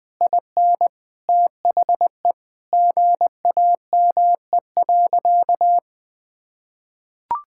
Morse code, heard as a keyed tone, about 20 wpm, 700 Hz. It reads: IN THE GAME.
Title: Sentences from Top 1000 Words - Review Part 6 - 20wpm